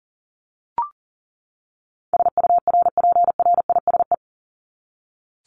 Morse code, heard as a keyed tone, about 40 wpm, 700 Hz. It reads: SURPRISE
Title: Top 1751-2000 Words 40wpm